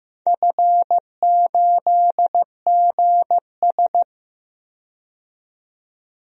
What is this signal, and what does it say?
Morse code, heard as a keyed tone, about 15 wpm, 700 Hz. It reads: F8GS
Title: DXCC-100 Call Signs - 4 Characters - 15wpm